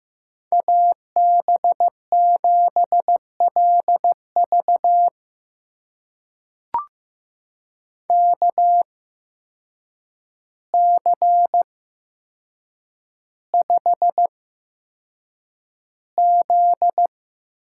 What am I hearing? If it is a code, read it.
Morse code, heard as a keyed tone, about 15 wpm, spaced as for 7 wpm, 700 Hz. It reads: AB7LV KC5Z
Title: USA Call Signs - 5 Characters - 8x Character Spacing - 15wpm